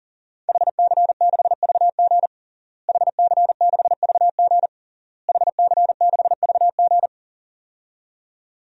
Morse code, heard as a keyed tone, about 37 wpm, 700 Hz. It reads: HC6VG HC6VG HC6VG
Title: DXCC-100 Call Signs - 5 Characters - Repeated 3x - 40wpm